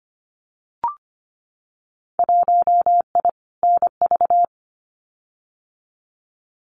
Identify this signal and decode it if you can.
Morse code, heard as a keyed tone, about 25 wpm, 700 Hz. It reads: 1I NV
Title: Winter Field Day Exchanges Encore 25wpm